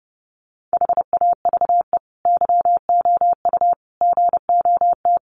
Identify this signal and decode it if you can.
Morse code, heard as a keyed tone, about 28 wpm, 700 Hz. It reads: HAVE YOU GOT
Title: Sentences from Top 300 Words 30wpm